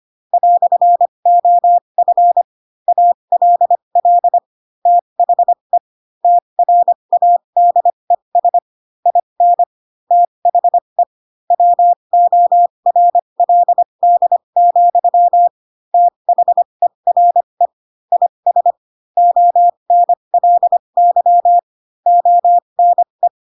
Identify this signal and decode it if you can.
Morse code, heard as a keyed tone, about 25 wpm, spaced as for 22 wpm, 700 Hz. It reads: "OF ALL THE TRADES IN THE WORLD, THERE IS ONLY ONE
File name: Pinocchio / Pinocchio ch4 18 25 ik1hge